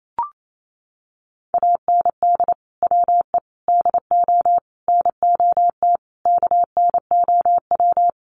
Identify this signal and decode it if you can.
Morse code, heard as a keyed tone, about 28 wpm, 700 Hz. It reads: AND WE DO NOT KNOW
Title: Sets of 5 Words Encore 28wpm